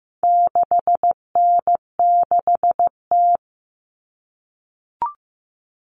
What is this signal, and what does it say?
Morse code, heard as a keyed tone, about 15 wpm, 700 Hz. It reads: 6N6T
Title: All Cut Numbers - 4-Digit 15wpm